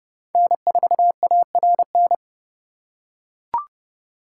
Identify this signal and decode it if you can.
Morse code, heard as a keyed tone, about 29 wpm, 700 Hz. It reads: N4ARN